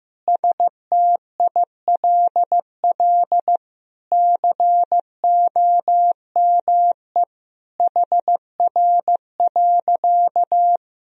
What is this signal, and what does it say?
Morse code, heard as a keyed tone, about 15 wpm, 700 Hz. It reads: STILL COME HR.
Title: Sentences from Top 300 Words and Abbreviations 15wpm